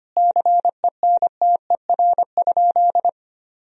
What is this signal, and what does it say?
Morse code, heard as a keyed tone, about 25 wpm, 700 Hz. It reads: CENTER?
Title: Sentences from Top 1000 Words - Review Part 2 - 25wpm